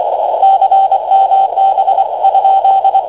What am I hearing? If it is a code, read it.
Morse code, heard as a keyed tone, about 25 wpm, 780 Hz, with radio noise. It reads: CMB ?
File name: w3df / ti9m 160